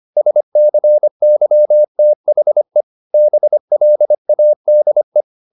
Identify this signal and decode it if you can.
Morse code, heard as a keyed tone, about 25 wpm, 600 Hz. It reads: SCYTHE BLADE